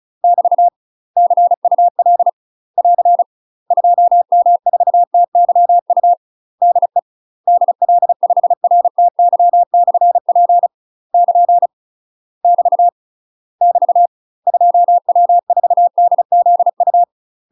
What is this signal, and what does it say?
Morse code, heard as a keyed tone, about 34 wpm, 700 Hz. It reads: <BT> CUL <AR> 2M4TYU DE DL5RTY/P <KN> <BT> <BT> 2W4DZU
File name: UK MASTER QSOs TRAINING / 35WPM / UK MASTER QSOs 0039